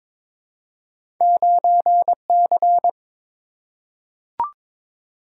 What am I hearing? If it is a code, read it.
Morse code, heard as a keyed tone, about 22 wpm, 700 Hz. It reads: 9C
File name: practice / 22wpm / Two Letter-Number Combinations - 22wpm